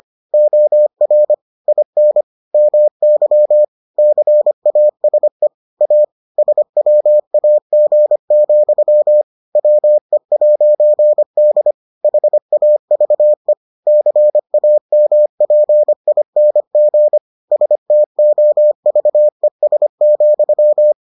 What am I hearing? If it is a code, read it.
Morse code, heard as a keyed tone, about 25 wpm, 600 Hz. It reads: OR IN MY CASE A SWAG, WE'D HAVE CAMPING STOVES,